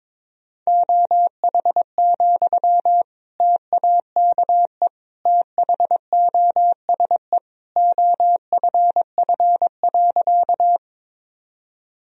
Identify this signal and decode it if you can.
Morse code, heard as a keyed tone, about 22 wpm, 700 Hz. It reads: OH, TAKE THOSE OFF.
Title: Sentences from Top 1000 Words - Review Part 1 - 22wpm